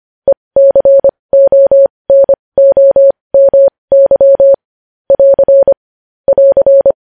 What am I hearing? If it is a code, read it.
Morse code, heard as a keyed tone, about 25 wpm, spaced as for 20 wpm, 550 Hz. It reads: ECONOMY <AR> <AR>
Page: news headlines in morse code at 20 wpm